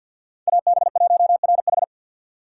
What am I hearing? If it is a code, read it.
Morse code, heard as a keyed tone, about 51 wpm, 700 Hz. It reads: AB1RH
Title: USA Call Signs - 5 Characters - 2x Character Spacing - 50wpm